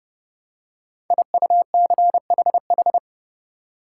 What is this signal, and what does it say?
Morse code, heard as a keyed tone, about 30 wpm, 700 Hz. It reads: IUCHH
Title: Code Groups - 5 Letters - 30wpm